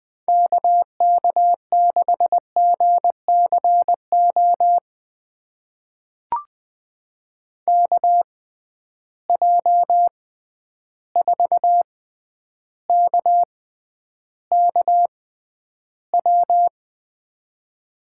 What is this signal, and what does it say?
Morse code, heard as a keyed tone, about 20 wpm, spaced as for 10 wpm, 700 Hz. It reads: KK6GCO KJ4KKW
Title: USA Call Signs - 6 Characters - 6x Character Spacing - 20wpm